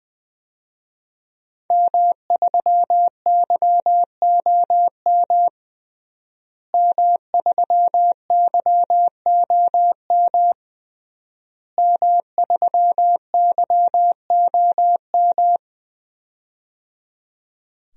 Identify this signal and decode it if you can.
Morse code, heard as a keyed tone, about 20 wpm, 700 Hz. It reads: M3YOM M3YOM M3YOM
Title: DXCC-100 Call Signs - 5 Characters - Repeated 3x - 20wpm